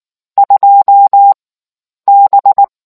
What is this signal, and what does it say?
Morse code, heard as a keyed tone, about 19 wpm, spaced as for 9 wpm, 800 Hz. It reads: 2B